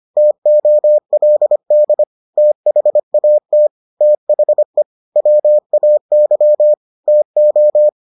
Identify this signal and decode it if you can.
Morse code, heard as a keyed tone, about 25 wpm, 600 Hz. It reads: TOLD THAT THE WAY TO